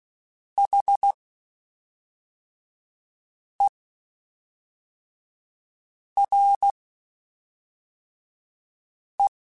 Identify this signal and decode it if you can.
Morse code, heard as a keyed tone, about 16 wpm, spaced as for 3 wpm, 800 Hz. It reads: HERE
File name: (spaced) / QSOs Spaced-mp3s / kc7haa @5 sp